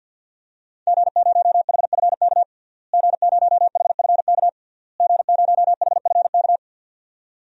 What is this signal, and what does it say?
Morse code, heard as a keyed tone, about 47 wpm, 700 Hz. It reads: G0HFK G0HFK G0HFK
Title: DXCC-100 Call Signs - 5 Characters - Repeated 3x - 50wpm